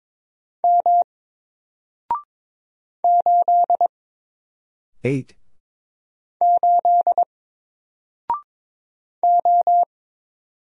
Morse code, heard as a keyed tone, about 22 wpm, 700 Hz. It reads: M 8 8 O
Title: Single Letter-Number Alt 22wpm